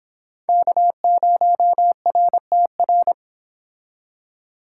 Morse code, heard as a keyed tone, about 26 wpm, 700 Hz. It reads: K0RTR
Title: Speed Racing - USA 1x3 Call Signs 22wpm